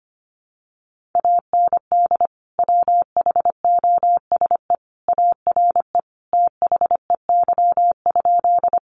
Morse code, heard as a keyed tone, about 25 wpm, 700 Hz. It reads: AND WHOSE ARE THEY?
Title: Sentences from Top 1200 Words 25wpm